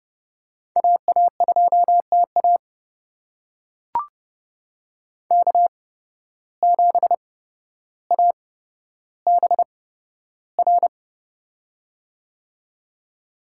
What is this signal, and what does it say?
Morse code, heard as a keyed tone, about 30 wpm, spaced as for 13 wpm, 700 Hz. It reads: AA2TA K7ABR
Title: USA Call Signs - 5 Characters - 8x Character Spacing - 30wpm